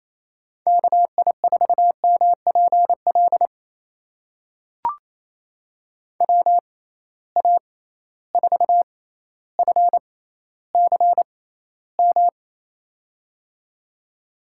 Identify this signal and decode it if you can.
Morse code, heard as a keyed tone, about 27 wpm, spaced as for 14 wpm, 700 Hz. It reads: KI4MPL WA4FCM
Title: USA Call Signs - 6 Characters - 6x Character Spacing - 28wpm